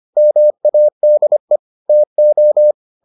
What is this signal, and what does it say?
Morse code, heard as a keyed tone, about 25 wpm, 600 Hz. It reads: MADE TO